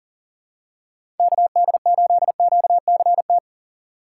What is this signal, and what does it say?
Morse code, heard as a keyed tone, about 37 wpm, 700 Hz. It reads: KD8QCT
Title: USA Call Signs - 6 Characters - 2x Character Spacing - 40wpm